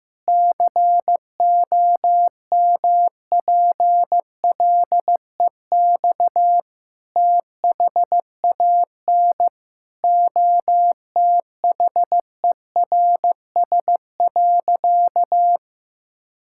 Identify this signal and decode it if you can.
Morse code, heard as a keyed tone, about 15 wpm, 700 Hz. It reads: COMPLEX THAN OTHERS.
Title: Sentences from Top 1000 Words - Review Part 6 - 15wpm